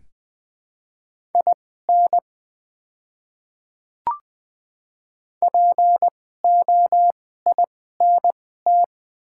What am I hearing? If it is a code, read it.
Morse code, heard as a keyed tone, about 20 wpm, spaced as for 14 wpm, 700 Hz. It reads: IN POINT
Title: 2x Character Spacing - Top 200 Words - 20wpm